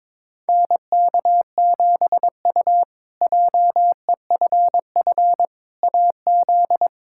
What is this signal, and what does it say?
Morse code, heard as a keyed tone, about 21 wpm, 700 Hz. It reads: NK7U JEFF AZ
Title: K1USN SST QSO Practice - 22wpm